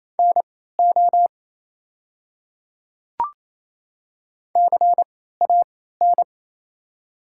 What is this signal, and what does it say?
Morse code, heard as a keyed tone, about 28 wpm, spaced as for 16 wpm, 700 Hz. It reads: NO CAN
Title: 3x Character Spacing - Top 100 Words - 28wpm